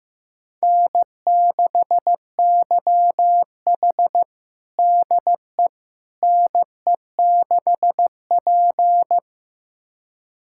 Morse code, heard as a keyed tone, about 15 wpm, 700 Hz. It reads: N6YH DE NE6P
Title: USA Callsign DE Callsign - 4 Characters - 15wpm